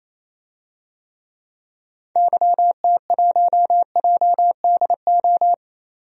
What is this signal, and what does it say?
Morse code, heard as a keyed tone, about 28 wpm, 700 Hz. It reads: YT1JDO
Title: DXCC-100 Call Signs - 6 Characters - 28wpm